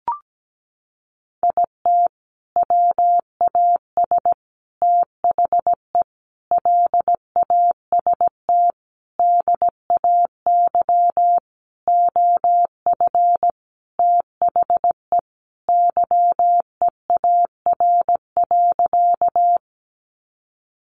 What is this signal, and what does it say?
Morse code, heard as a keyed tone, about 17 wpm, 700 Hz. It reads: IT WAS THE LAST DAY OF THE YEAR.
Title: Sentences from Top 300 Words 17wpm